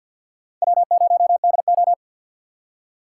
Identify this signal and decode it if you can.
Morse code, heard as a keyed tone, about 52 wpm, 700 Hz. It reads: W0DO
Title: USA Call Signs - 4 Characters - 3x Character Spacing - 50wpm